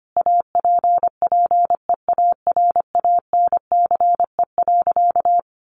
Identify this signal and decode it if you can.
Morse code, heard as a keyed tone, about 25 wpm, 700 Hz. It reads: APPEARANCE.
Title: Sentences from Top 1100 Words 25wpm